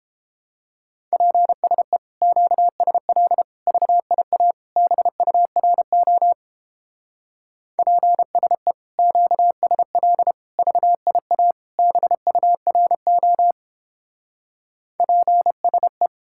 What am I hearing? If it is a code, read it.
Morse code, heard as a keyed tone, about 29 wpm, 700 Hz. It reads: PSE QSL VIA BURO PSE QSL VIA BURO PSE